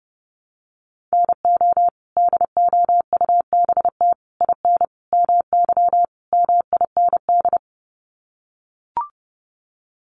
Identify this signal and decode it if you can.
Morse code, heard as a keyed tone, about 30 wpm, 700 Hz. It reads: NO DOUBT IN MY MIND
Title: Sets of 5 Words Encore2 30wpm